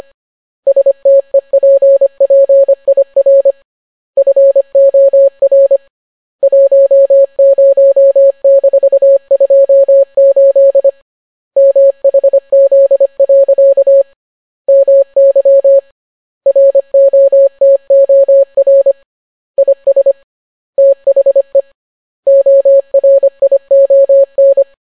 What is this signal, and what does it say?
Morse code, heard as a keyed tone, about 25 wpm, 550 Hz. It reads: STEPPIR FOR 10-28 MHZ. MY ROTOR IS THE ORION